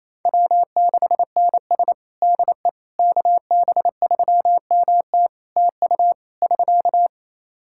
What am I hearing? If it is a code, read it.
Morse code, heard as a keyed tone, about 27 wpm, 700 Hz. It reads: W6NS DE KB3MT TU <SK>
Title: Practice QSO - Exchange 4 - Ending - 28wpm